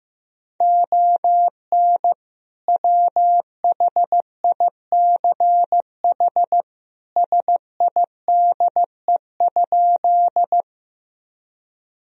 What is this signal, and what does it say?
Morse code, heard as a keyed tone, about 15 wpm, 700 Hz. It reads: ON WHICH SIDE?
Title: Sentences from Top 1000 Words - Review Part 1 - 15wpm